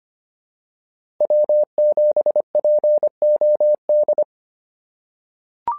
Morse code, heard as a keyed tone, about 25 wpm, 600 Hz. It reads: W7POD
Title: SOTA QSO Practice 25wpm